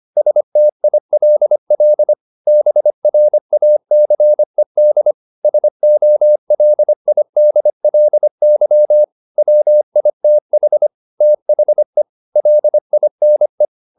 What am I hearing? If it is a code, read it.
Morse code, heard as a keyed tone, about 25 wpm, 600 Hz. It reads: STILL BRACED SOLIDLY WITH THE LINE
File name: the old man and the sea part 1 25wpm